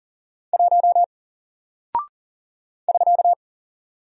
Morse code, heard as a keyed tone, about 40 wpm, 700 Hz. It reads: 1 <SK>
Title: Single Character (Letter:Number:Punctuation:Prosign) Alt 40wpm